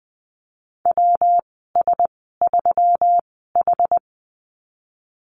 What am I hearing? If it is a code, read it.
Morse code, heard as a keyed tone, about 20 wpm, spaced as for 15 wpm, 700 Hz. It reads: WS3H
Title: 122 Lesson 20g - Number 3 - Call Signs 20wpm